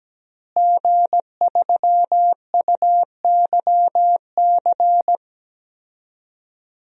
Morse code, heard as a keyed tone, about 17 wpm, 700 Hz. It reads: G3UYC
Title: DXCC-100 Call Signs - 5 Characters - Repeated 3x - 17wpm